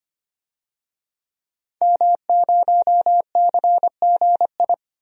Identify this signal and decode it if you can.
Morse code, heard as a keyed tone, about 25 wpm, 700 Hz. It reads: M0CGI
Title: DXCC-100 Call Signs - 5 Characters - 25wpm